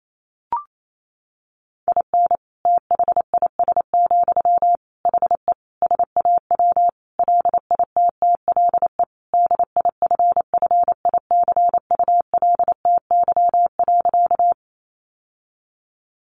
Morse code, heard as a keyed tone, about 28 wpm, 700 Hz. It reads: IN THIS, HE SAW LITTLE DIFFICULTY.
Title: Sentences from Top 1500 Words 28wpm